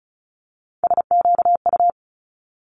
Morse code, heard as a keyed tone, about 36 wpm, 700 Hz. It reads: SQU
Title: Multi-letter Phonemes Quick Fire 35wpm